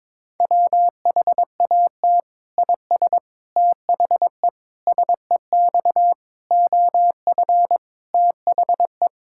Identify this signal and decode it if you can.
Morse code, heard as a keyed tone, about 22 wpm, 700 Hz. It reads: WHAT IS THE SEX OF THE
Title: Sentences from Top 1600 Words 22wpm